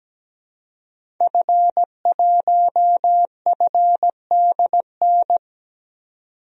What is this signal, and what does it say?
Morse code, heard as a keyed tone, about 17 wpm, 700 Hz. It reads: F1FDN
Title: DXCC-100 Call Signs - 5 Characters - 17wpm